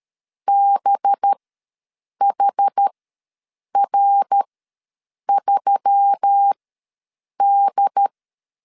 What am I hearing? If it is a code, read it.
Morse code, heard as a keyed tone, about 13 wpm, spaced as for 7 wpm, 800 Hz. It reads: BHR3D
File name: LEZIONI / 9 286 ws 12wpm 3351